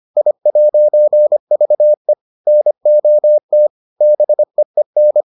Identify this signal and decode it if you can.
Morse code, heard as a keyed tone, about 25 wpm, 600 Hz. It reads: I'VE NOT BEEN